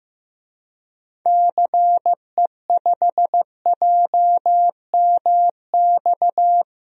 Morse code, heard as a keyed tone, about 15 wpm, 700 Hz. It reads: CE5JMX
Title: DXCC-100 Call Signs - 6 Characters - Repeated 2x - 15wpm